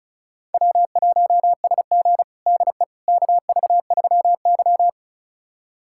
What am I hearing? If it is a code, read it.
Morse code, heard as a keyed tone, about 36 wpm, 700 Hz. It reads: W1SG DE KV3Y
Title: USA Callsign DE Callsign - 4 Characters - 35wpm